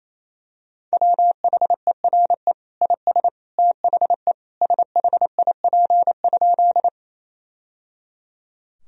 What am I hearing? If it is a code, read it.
Morse code, heard as a keyed tone, about 28 wpm, 700 Hz. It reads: WHERE IS THE SHIP?
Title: Sentences from Top 1000 Words - Review Part 2 - 28wpm